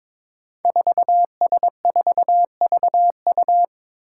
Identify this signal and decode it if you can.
Morse code, heard as a keyed tone, about 23 wpm, 700 Hz. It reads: 4S4VU
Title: Five Challenging Letter-Number Combinations 22wpm